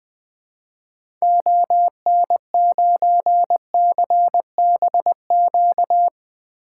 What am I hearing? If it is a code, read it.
Morse code, heard as a keyed tone, about 20 wpm, 700 Hz. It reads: ON9CBQ
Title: 2023 SOTA Activator Call Signs - 20wpm